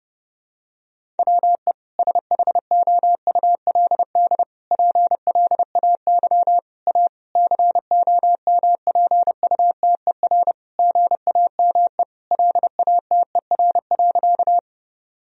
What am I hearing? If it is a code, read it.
Morse code, heard as a keyed tone, about 30 wpm, 700 Hz. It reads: WE SHOULD PLAY A COMPUTER GAME LATER.
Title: Sentences from Top 1000 Words - Review Part 3 - 30wpm